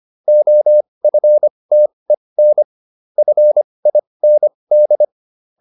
Morse code, heard as a keyed tone, about 25 wpm, spaced as for 20 wpm, 600 Hz. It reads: OFTEN FIND
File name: SS206 20